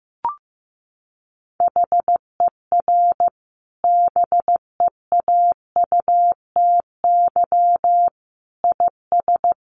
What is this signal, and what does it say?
Morse code, heard as a keyed tone, about 15 wpm, 700 Hz. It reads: HER BEAUTY IS
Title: Sentences from Top 1100 Words 15wpm